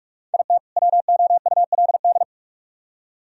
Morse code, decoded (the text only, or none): IT WOULD